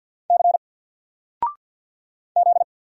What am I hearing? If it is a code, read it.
Morse code, heard as a keyed tone, about 48 wpm, 700 Hz. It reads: C Z
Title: Single Character (Letter:Number:Punctuation:Prosign) Alt 50wpm